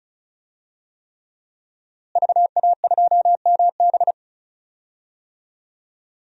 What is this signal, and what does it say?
Morse code, heard as a keyed tone, about 36 wpm, 700 Hz. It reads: VA2MB
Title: DXCC-100 Call Signs - 5 Characters - Repeated 3x - 35wpm